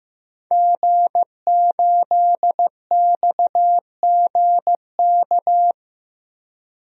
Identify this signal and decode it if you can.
Morse code, heard as a keyed tone, about 15 wpm, 700 Hz. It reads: G8XGK
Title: DXCC-100 Call Signs - 5 Characters - 15wpm